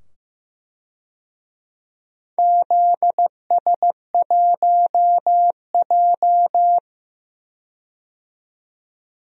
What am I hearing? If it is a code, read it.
Morse code, heard as a keyed tone, about 15 wpm, 700 Hz. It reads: ZS1J
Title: DXCC-100 Call Signs - 4 Characters - 15wpm